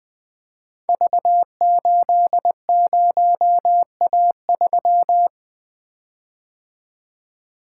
Morse code, heard as a keyed tone, about 20 wpm, 700 Hz. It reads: V80A3
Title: Code Groups - 5 Letter:Numbers - 20wpm